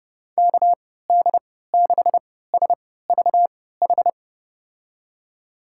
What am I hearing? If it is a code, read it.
Morse code, heard as a keyed tone, about 29 wpm, spaced as for 17 wpm, 700 Hz. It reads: KD6SVH
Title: USA Call Signs - 6 Characters - 3x Character Spacing - 30wpm